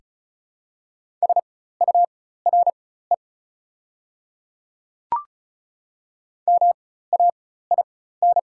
Morse code, heard as a keyed tone, about 34 wpm, spaced as for 16 wpm, 690 Hz. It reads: SURE MAIN
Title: 4x Character Spacing - Top 500 Words - 35wpm